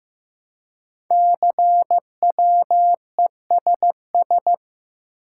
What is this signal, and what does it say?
Morse code, heard as a keyed tone, about 15 wpm, 700 Hz. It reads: CWESS
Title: Code Groups - 5 Letters - 15wpm